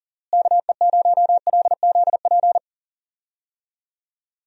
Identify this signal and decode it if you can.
Morse code, heard as a keyed tone, about 37 wpm, 700 Hz. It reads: KE0LZP
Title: USA Call Signs - 6 Characters - Repeated 3x - 40wpm